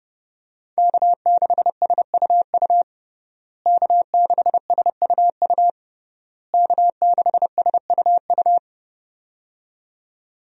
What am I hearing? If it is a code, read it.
Morse code, heard as a keyed tone, about 30 wpm, 700 Hz. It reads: K6SUU K6SUU K6SUU
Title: USA Call Signs - 5 Characters - Repeated 3x - 30wpm